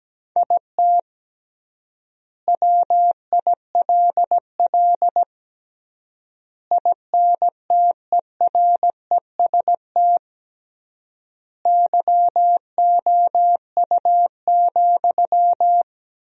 Morse code, heard as a keyed tone, about 17 wpm, 700 Hz. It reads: IT WILL INTEREST YOU,
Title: 3x Word-Spacing - Sentences from Top 400 Words - 17wpm